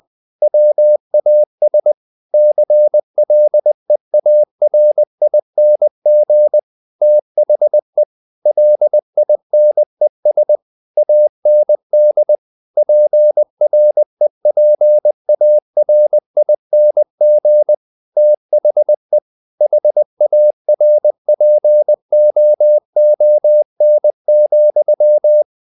Morse code, heard as a keyed tone, about 20 wpm, 600 Hz. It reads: WAS CLEARING THE LINES AND PREPARING THE HARPOON,
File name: the old man and the sea / the old man and the sea part 1 20wpm